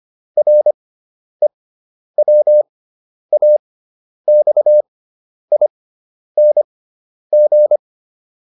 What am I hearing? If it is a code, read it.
Morse code, heard as a keyed tone, about 25 wpm, spaced as for 10 wpm, 600 Hz. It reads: REWAXING